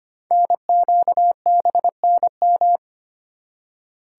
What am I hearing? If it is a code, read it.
Morse code, heard as a keyed tone, about 25 wpm, 700 Hz. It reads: NQBNM
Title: Code Groups - 5 Letters - 25wpm